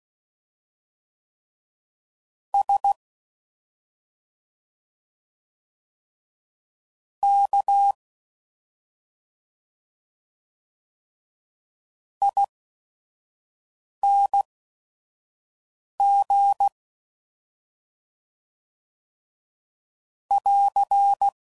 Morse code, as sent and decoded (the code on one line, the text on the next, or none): ... / -.- / .. -. --. / .-.-.
S K ING <AR>